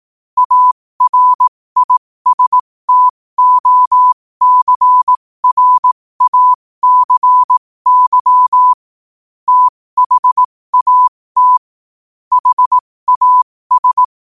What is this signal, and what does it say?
Morse code, heard as a keyed tone, about 18 wpm, spaced as for 15 wpm, 1000 Hz. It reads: ARISTOCRACY THAT HAS